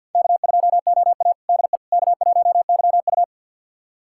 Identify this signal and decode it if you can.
Morse code, heard as a keyed tone, about 50 wpm, 700 Hz. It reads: K2OA DE K1YU
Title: USA Callsign DE Callsign - 4 Characters - 50wpm